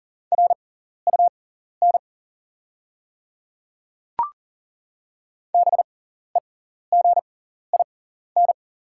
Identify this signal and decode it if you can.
Morse code, heard as a keyed tone, about 40 wpm, spaced as for 14 wpm, 700 Hz. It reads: RUN BEGIN
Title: 6x Character Spacing - Top 300 Words - 40wpm